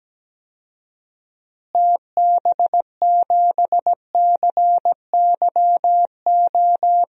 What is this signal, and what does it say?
Morse code, heard as a keyed tone, about 17 wpm, 700 Hz. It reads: TB7CYO
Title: DXCC-100 Call Signs - 6 Characters - Repeated 3x - 17wpm